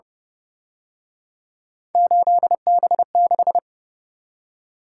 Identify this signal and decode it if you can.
Morse code, heard as a keyed tone, about 29 wpm, 700 Hz. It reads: 8B6
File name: Speed-Racing - Three Letter-Number Combinations - 25wpm